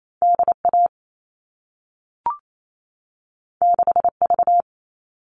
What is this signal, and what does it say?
Morse code, heard as a keyed tone, about 28 wpm, 700 Hz. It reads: DA 6V
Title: All Cut Numbers - 2-Digit 28wpm